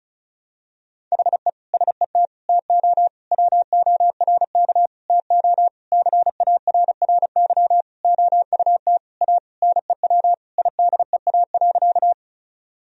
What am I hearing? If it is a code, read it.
Morse code, heard as a keyed tone, about 34 wpm, 700 Hz. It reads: HE SET TO WORK TO CARRY OUT A NEW IDEA.
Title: Sentences from Top 1000 Words - Review Part 1 - 35wpm